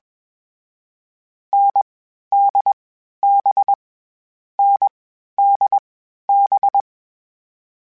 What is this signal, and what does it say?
Morse code, heard as a keyed tone, about 21 wpm, 800 Hz. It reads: N D B N D B
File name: Variable Speed / Variable Speed-mp3s / GZ GZ7 sp @20